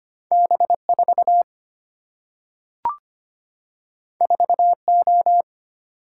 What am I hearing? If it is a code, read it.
Morse code, heard as a keyed tone, about 25 wpm, 700 Hz. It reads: B4 4O
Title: Single-Letter Single-Number 25wpm